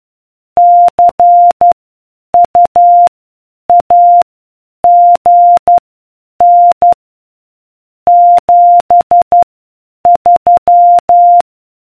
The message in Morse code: CUAGN 73